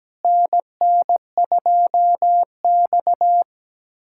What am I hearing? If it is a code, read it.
Morse code, heard as a keyed tone, about 17 wpm, 700 Hz. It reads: NN2X
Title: USA Callsign DE Callsign - 4 Characters - 17wpm